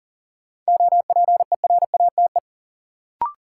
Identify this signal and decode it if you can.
Morse code, heard as a keyed tone, about 40 wpm, 700 Hz. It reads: OPERATE
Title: Sentences from Top 1500 Words - New Word Drill Alt - 40wpm